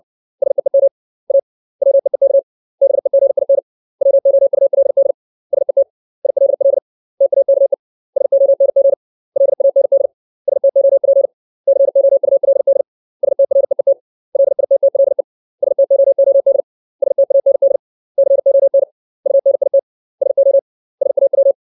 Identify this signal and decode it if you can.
Morse code, heard as a keyed tone, about 60 wpm, 550 Hz. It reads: SEEM A WEEK BEGIN WOULD SET ILL TAKE SONG LAND STOP COULD STREET LITTLE STOOD STAND CON UNIT SO SAW
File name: 632 words / 632 Words 60WPM